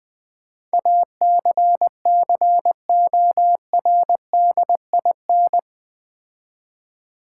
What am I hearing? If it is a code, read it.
Morse code, heard as a keyed tone, about 20 wpm, 700 Hz. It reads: ACCORDIN
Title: Build A Word - 4 Letters At A Time - 11-Letter Words - 20wpm